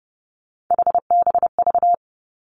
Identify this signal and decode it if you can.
Morse code, heard as a keyed tone, about 30 wpm, 700 Hz. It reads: HBV
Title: Three Challenging Letter-Number Combinations 30wpm